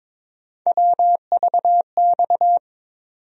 Morse code, heard as a keyed tone, about 22 wpm, 700 Hz. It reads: WVX
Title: USA Call Sign Postfixes - 3 Letters 22wpm